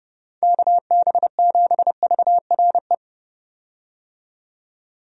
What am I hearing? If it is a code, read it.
Morse code, heard as a keyed tone, about 32 wpm, 700 Hz. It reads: KB7VRE
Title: USA Call Signs - 6 Characters - 30wpm